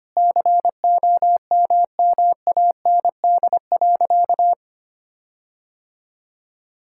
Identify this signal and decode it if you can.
Morse code, heard as a keyed tone, about 25 wpm, 700 Hz. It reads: COMMAND.
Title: Sentences from Top 1600 Words 25wpm